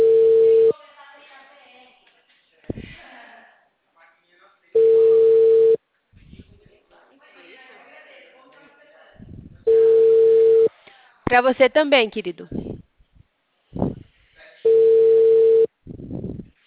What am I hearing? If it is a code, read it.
Ringback tone (China, 450 Hz), with speech.
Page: Sorocaba - SP